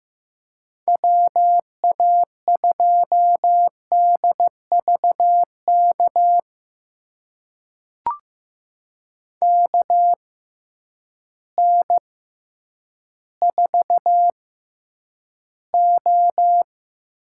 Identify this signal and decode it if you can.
Morse code, heard as a keyed tone, about 15 wpm, spaced as for 9 wpm, 700 Hz. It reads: WA2DVK KN4O
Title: USA Call Signs - 6 Characters - 6x Character Spacing - 15wpm